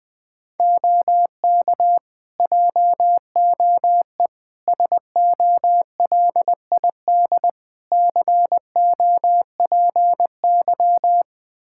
Morse code, heard as a keyed tone, about 20 wpm, 700 Hz. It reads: OK JOE SOLID COPY
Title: Practice QSO - Exchange 3 - More Info - 20wpm